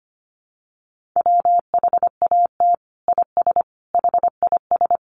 Morse code, heard as a keyed tone, about 25 wpm, 700 Hz. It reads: WHAT IS HIS
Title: Sentences from Top 200 Words 25wpm